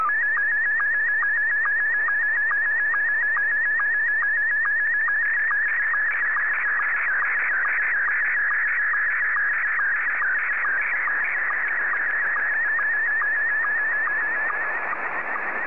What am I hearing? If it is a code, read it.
SSTV (slow-scan television), Scottie 1.